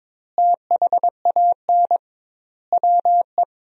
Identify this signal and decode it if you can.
Morse code, heard as a keyed tone, about 21 wpm, 700 Hz. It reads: THAN WE
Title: 2x Word-Spacing - Sentences from Top 100 Words - 22wpm